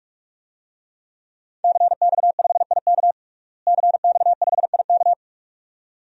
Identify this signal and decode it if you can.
Morse code, heard as a keyed tone, about 46 wpm, 700 Hz. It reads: CX5IK CX5IK
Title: DXCC-100 Call Signs - 5 Characters - Repeated 2x - 45wpm